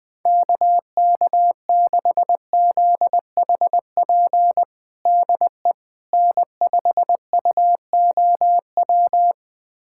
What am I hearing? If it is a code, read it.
Morse code, heard as a keyed tone, about 21 wpm, 700 Hz. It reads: KK6ZHP DE N5UOW
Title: USA Callsign DE Callsign 20wpm